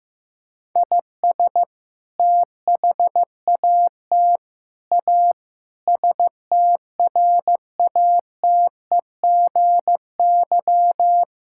Morse code, heard as a keyed tone, about 15 wpm, 700 Hz. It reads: IS THAT A STRATEGY
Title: Sentences from Top 1000 Words - Review Part 5 - 15wpm